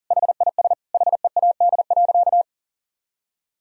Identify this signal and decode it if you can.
Morse code, heard as a keyed tone, about 40 wpm, 700 Hz. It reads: HIS HEAD.
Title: Sentences from Top 1000 Words - Review Part 3 - 40wpm